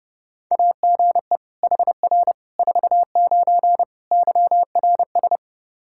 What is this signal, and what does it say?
Morse code, heard as a keyed tone, about 30 wpm, 700 Hz. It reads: AGE HR 49 YRS
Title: Practice QSO - Exchange 3 - More Info - 30wpm